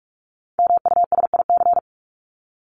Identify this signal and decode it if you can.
Morse code, heard as a keyed tone, about 44 wpm, 700 Hz. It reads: MUSIC
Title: Top 300 Words 45wpm